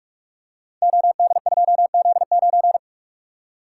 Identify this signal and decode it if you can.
Morse code, heard as a keyed tone, about 44 wpm, 700 Hz. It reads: OD2Z9
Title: Code Groups - 5 Letter:Numbers - 45wpm